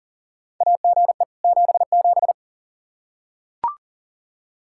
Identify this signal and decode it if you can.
Morse code, heard as a keyed tone, about 40 wpm, 700 Hz. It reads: AGE 77
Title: Practice QSO - Exchange 3 - More Info - 40wpm